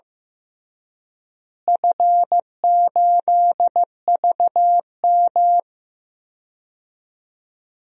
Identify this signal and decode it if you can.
Morse code, heard as a keyed tone, about 15 wpm, 700 Hz. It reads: F8VM
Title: DXCC-100 Call Signs - 4 Characters - Repeated 3x - 15wpm